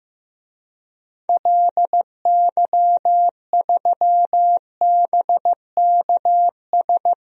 Morse code, heard as a keyed tone, about 15 wpm, 700 Hz. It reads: LY3BKS
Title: DXCC-100 Call Signs - 6 Characters - Repeated 3x - 15wpm